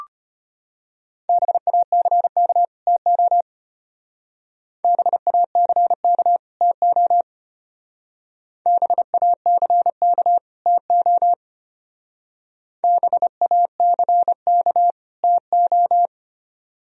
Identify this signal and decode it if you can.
Morse code, heard as a keyed tone, about 31 wpm, 700 Hz. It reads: BACK TO BACK TO BACK TO BACK TO
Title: Speed-Racing - Sets of 2 Words 25wpm